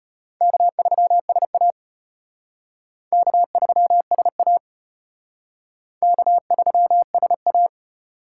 Morse code, heard as a keyed tone, about 34 wpm, 700 Hz. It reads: K3SA K3SA K3SA